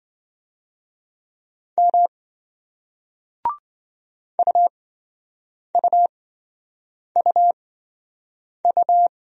Morse code, heard as a keyed tone, about 28 wpm, spaced as for 7 wpm, 700 Hz. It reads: M UUUU